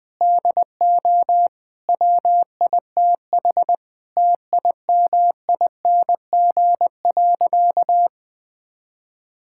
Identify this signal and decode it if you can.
Morse code, heard as a keyed tone, about 20 wpm, 700 Hz. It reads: DO WITH TIMING.
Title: Sentences with 8-Word Phrases - Spoken Beginning - 20wpm